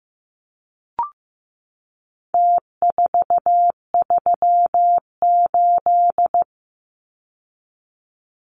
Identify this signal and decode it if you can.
Morse code, heard as a keyed tone, about 15 wpm, 700 Hz. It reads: T438
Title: Numbers - Contest Style - 4-Digit 15wpm